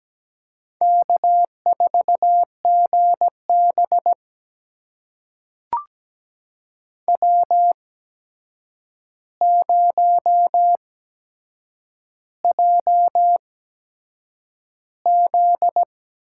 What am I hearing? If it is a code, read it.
Morse code, heard as a keyed tone, about 17 wpm, spaced as for 7 wpm, 700 Hz. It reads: K4GB W0JZ
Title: USA Call Signs - 4 Characters - 8x Character Spacing - 17wpm